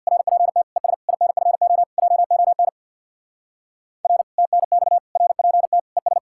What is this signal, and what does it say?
Morse code, heard as a keyed tone, about 59 wpm, 700 Hz. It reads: RPT ES INFO JON R TNX RPT ES
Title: Speed-Racing Common QSO Phrases - 50wpm